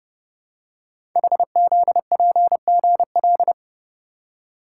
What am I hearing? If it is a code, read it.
Morse code, heard as a keyed tone, about 30 wpm, 700 Hz. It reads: HZPGL
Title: Code Groups - 5 Letters - 30wpm